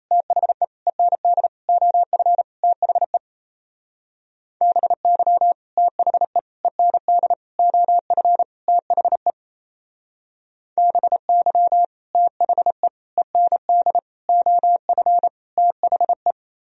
Morse code, heard as a keyed tone, about 36 wpm, 700 Hz. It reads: THE END OF THE BY THE END OF THE BY THE END OF THE